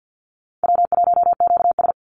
Morse code, heard as a keyed tone, about 50 wpm, 700 Hz. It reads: W1QS
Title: Easy Call Signs Repeated 3x - 50wpm